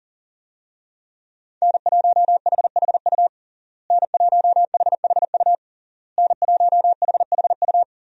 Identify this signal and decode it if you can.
Morse code, heard as a keyed tone, about 40 wpm, 700 Hz. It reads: N1HHU N1HHU N1HHU
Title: USA Call Signs - 5 Characters - Repeated 3x - 40wpm